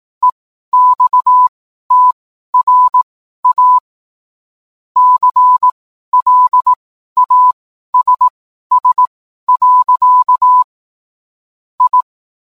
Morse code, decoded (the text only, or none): EXTRA CLASS. I